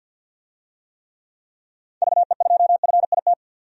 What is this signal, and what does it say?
Morse code, heard as a keyed tone, about 53 wpm, 700 Hz. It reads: VE2FIT
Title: DXCC-100 Call Signs - 6 Characters - 50wpm